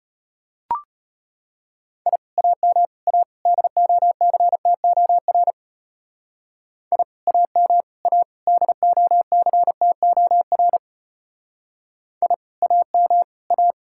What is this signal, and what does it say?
Morse code, heard as a keyed tone, about 31 wpm, 700 Hz. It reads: I AM A DOCTOR I AM A DOCTOR I AM A
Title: Speed-Racing Common QSO Phrases - 25wpm